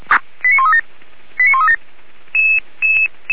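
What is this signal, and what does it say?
Five-tone selective call (ZVEI), heard as quick runs of five tones, tones of about 70 ms. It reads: 89137 89137
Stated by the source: ZVEI five-tone call received by SDR, from server of Feuerwehr Orlamünde fire brigade.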